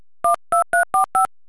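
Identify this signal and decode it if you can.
DTMF, heard as telephone touch tones, tones of about 120 ms. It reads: 12345
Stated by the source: DTMF tones for digits 1 2 3 4 5, mono, telephone multi-frequency dialing, by Friedrich Balck for digit-recognition training.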